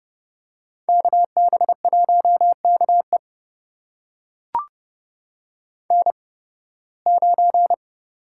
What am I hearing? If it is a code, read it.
Morse code, heard as a keyed tone, about 30 wpm, spaced as for 20 wpm, 700 Hz. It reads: KB1KE N9
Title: USA Call Signs - 5 Characters - 8x Character Spacing - 30wpm